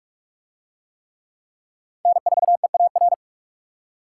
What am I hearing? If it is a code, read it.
Morse code, heard as a keyed tone, about 45 wpm, 700 Hz. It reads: N4EAR